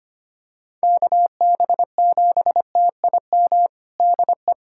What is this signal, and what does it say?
Morse code, heard as a keyed tone, about 25 wpm, 700 Hz. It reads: KB7TIM DE
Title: USA Callsign DE Callsign - 6 Characters - 25wpm